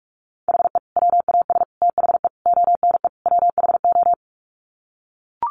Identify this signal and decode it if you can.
Morse code, heard as a keyed tone, about 44 wpm, 700 Hz. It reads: HE WAS THE ONE WHO
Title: Sets of 5 Words Encore2 45wpm